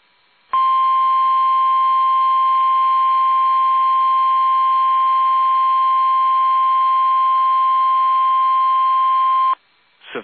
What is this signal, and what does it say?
Emergency Alert System (SAME) tones (the 1050 Hz attention tone). It is heard from 0.5 to 9.6 s.
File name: eas / KUSO / KUSO